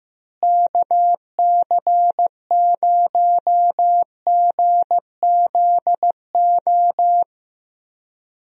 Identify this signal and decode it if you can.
Morse code, heard as a keyed tone, about 15 wpm, 700 Hz. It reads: KC0GZO
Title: USA Call Signs - 6 Characters - Repeated 2x - 15wpm